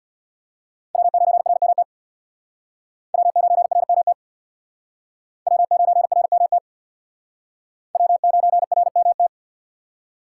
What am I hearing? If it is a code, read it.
Morse code, heard as a keyed tone, about 60 wpm, 700 Hz. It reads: W9RMT W9RMT W9RMT W9RMT
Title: Speed Racing - USA 1x3 Call Signs 50wpm